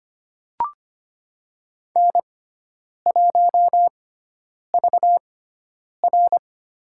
Morse code, heard as a keyed tone, about 25 wpm, spaced as for 9 wpm, 700 Hz. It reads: N1VR
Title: USA Call Signs - 6x Character Spacing - 25wpm